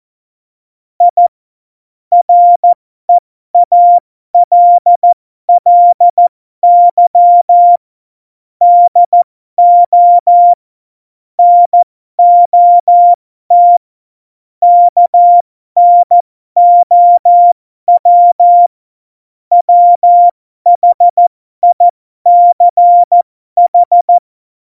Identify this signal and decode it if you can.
Morse code, heard as a keyed tone, about 14 wpm, spaced as for 12 wpm, 700 Hz. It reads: I REALLY DO NOT KNOW WHICH
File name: Morsekurs / Witze